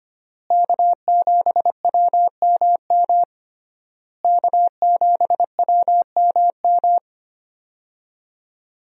Morse code, heard as a keyed tone, about 25 wpm, 700 Hz. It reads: K7WMM K7WMM
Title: USA Call Signs - 5 Characters - Repeated 3x - 25wpm